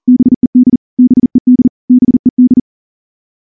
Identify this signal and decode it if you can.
Morse code, heard as a keyed tone, about 38 wpm, 250 Hz. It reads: BED BED BED
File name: morse / morse-training-40WPM